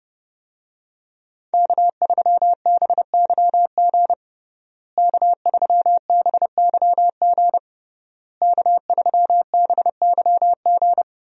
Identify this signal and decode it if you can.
Morse code, heard as a keyed tone, about 30 wpm, 700 Hz. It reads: K3BYG K3BYG K3BYG
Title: USA Call Signs - 5 Characters - Repeated 3x - 30wpm